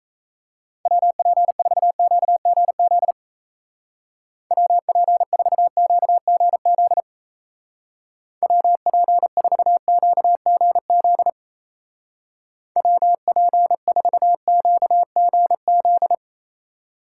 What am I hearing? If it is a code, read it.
Morse code, heard as a keyed tone, about 36 wpm, 700 Hz. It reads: WP4QGZ WP4QGZ WP4QGZ WP4QGZ